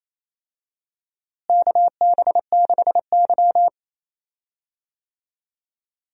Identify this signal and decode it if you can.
Morse code, heard as a keyed tone, about 28 wpm, 700 Hz. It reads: KB6Y
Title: USA Call Signs - 4 Characters - Repeated 3x - 28wpm